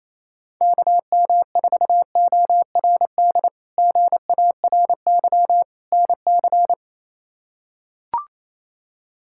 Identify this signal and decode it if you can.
Morse code, heard as a keyed tone, about 28 wpm, 700 Hz. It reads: KM4ORD GARY NC
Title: K1USN SST QSO Practice - 28wpm